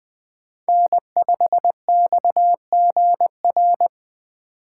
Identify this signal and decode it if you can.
Morse code, heard as a keyed tone, about 20 wpm, 700 Hz. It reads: N5XGR
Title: USA Call Signs - 5 Characters - 4x Character Spacing - 20wpm